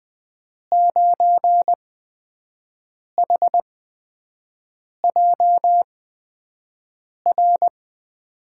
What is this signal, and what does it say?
Morse code, heard as a keyed tone, about 21 wpm, spaced as for 5 wpm, 700 Hz. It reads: 9HJR
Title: USA Call Signs - 6 Characters - 8x Character Spacing - 20wpm